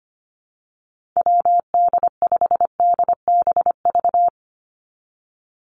Morse code, heard as a keyed tone, about 25 wpm, 700 Hz. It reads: WD5DBV
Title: Call Signs Repeated 2x - 25wpm